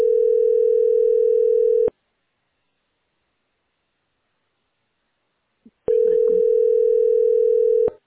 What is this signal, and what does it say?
Ringback tone (North America), 440 + 480 Hz.